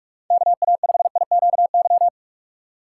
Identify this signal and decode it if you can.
Morse code, heard as a keyed tone, about 46 wpm, 700 Hz. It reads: KA5IQY
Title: USA Call Signs - 6 Characters - 3x Character Spacing - 45wpm